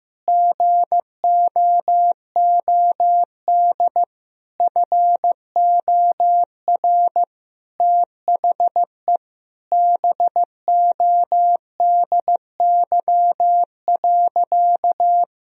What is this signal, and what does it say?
Morse code, heard as a keyed tone, about 15 wpm, 700 Hz. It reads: GOOD FOR THE BODY.
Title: Sentences from Top 1000 Words - Review Part 4 - 15wpm